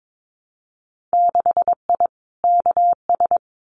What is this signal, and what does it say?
Morse code, heard as a keyed tone, about 22 wpm, 700 Hz. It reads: 6I KS
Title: Winter Field Day Exchanges Encore 22wpm